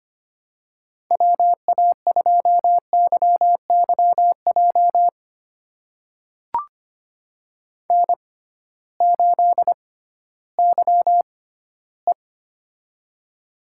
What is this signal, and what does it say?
Morse code, heard as a keyed tone, about 25 wpm, spaced as for 15 wpm, 700 Hz. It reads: WA2YYJ N8YE
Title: USA Call Signs - 6x Character Spacing - 25wpm